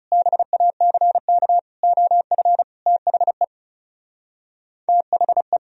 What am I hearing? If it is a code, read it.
Morse code, heard as a keyed tone, about 40 wpm, 700 Hz. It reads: BACK OF THE THE